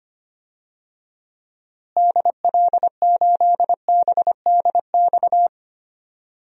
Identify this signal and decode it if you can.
Morse code, heard as a keyed tone, about 25 wpm, 700 Hz. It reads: DL8BDX
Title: DXCC-100 Call Signs - 6 Characters - Repeated 3x - 25wpm